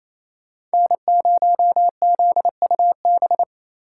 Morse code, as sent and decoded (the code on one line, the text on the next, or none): -. ----- --.. ..- -...
N0ZUB